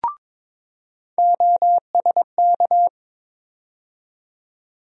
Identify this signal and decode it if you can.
Morse code, heard as a keyed tone, about 22 wpm, 700 Hz. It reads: OSK
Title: USA Call Sign Postfixes - 3 Letters 22wpm